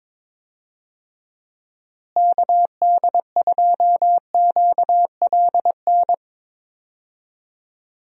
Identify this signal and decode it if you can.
Morse code, heard as a keyed tone, about 21 wpm, 700 Hz. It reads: KD2QLN
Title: USA Call Signs - 6 Characters - Repeated 2x - 22wpm